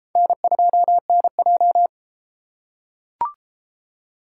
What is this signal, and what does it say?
Morse code, heard as a keyed tone, about 29 wpm, 700 Hz. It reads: N2NJ